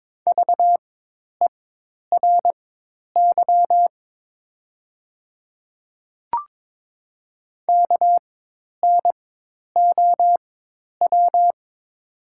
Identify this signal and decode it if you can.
Morse code, heard as a keyed tone, about 22 wpm, spaced as for 10 wpm, 700 Hz. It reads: VERY KNOW
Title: 4x Character Spacing - Top 300 Words - 22wpm